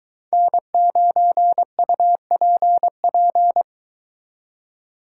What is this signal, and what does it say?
Morse code, heard as a keyed tone, about 23 wpm, 700 Hz. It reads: N9UPP